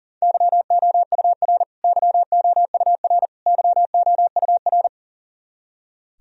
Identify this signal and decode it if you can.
Morse code, heard as a keyed tone, about 37 wpm, 700 Hz. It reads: YOUR YOUR YOUR
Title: Top 100 Words Repeated 40wpm